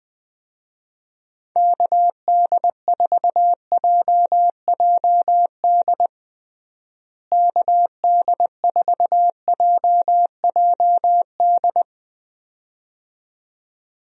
Morse code, heard as a keyed tone, about 21 wpm, 700 Hz. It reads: KD4JJD KD4JJD
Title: USA Call Signs - 6 Characters - Repeated 2x - 20wpm